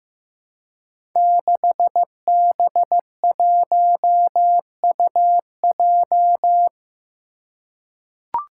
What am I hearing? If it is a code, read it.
Morse code, heard as a keyed tone, about 15 wpm, 700 Hz. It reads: 6B1UJ
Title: Five Challenging Letter-Number Combinations 15wpm